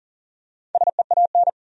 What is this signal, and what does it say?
Morse code, heard as a keyed tone, about 40 wpm, 700 Hz. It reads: SEAN
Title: USA Common Names - 40wpm